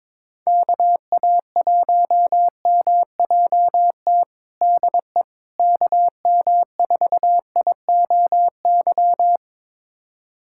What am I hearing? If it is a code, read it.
Morse code, heard as a keyed tone, about 22 wpm, 700 Hz. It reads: KA1MJT DE KM4IOY
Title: USA Callsign DE Callsign - 6 Characters - 22wpm